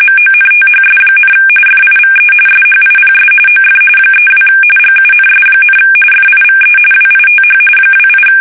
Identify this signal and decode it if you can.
RTTY, about 45.45 baud. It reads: MONTHS, FIVE SHIPS OF 32 GUNS, FIVE WITH 28
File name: fsk850 / usnavy1